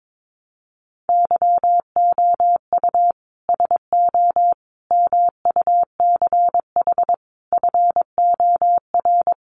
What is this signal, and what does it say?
Morse code, heard as a keyed tone, about 22 wpm, 700 Hz. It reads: YOU SO MUCH FOR
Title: Sets of 5 Words Encore2 22wpm